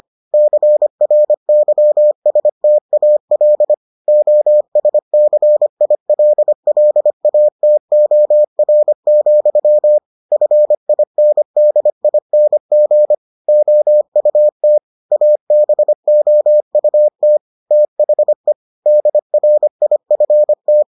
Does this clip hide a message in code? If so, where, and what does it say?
Yes: Morse code from 0.3 to 20.8 s, heard as a keyed tone, about 25 wpm, 600 Hz. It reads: CRYSTAL OSCILLATOR, FINDING OUT ABOUT THE DRIFT